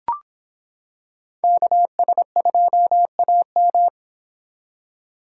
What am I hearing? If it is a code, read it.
Morse code, heard as a keyed tone, about 26 wpm, 700 Hz. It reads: KS2AM